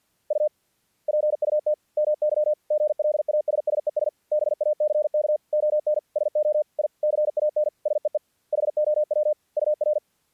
Morse code, heard as a keyed tone, about 48 wpm, 600 Hz. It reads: U PUT MY GLASSES BACK ON SO I CAN SEE HOW UR
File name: archived / CRSnet